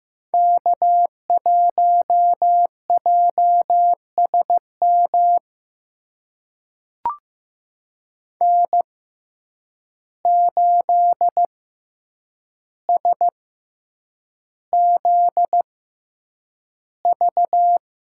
Morse code, heard as a keyed tone, about 15 wpm, spaced as for 8 wpm, 700 Hz. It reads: K1JSM N8SZV
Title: USA Call Signs - 5 Characters - 6x Character Spacing - 15wpm